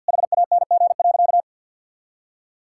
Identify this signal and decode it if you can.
Morse code, heard as a keyed tone, about 50 wpm, 700 Hz. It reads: HANG.
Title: Sentences from Top 1600 Words 50wpm